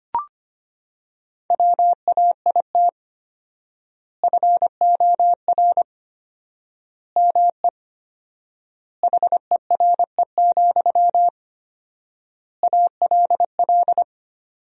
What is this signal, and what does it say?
Morse code, heard as a keyed tone, about 25 wpm, 700 Hz. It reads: WAIT FOR ME HERE, ALL
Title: 4x Word-Spacing - Sentences from Top 500 Words - 25wpm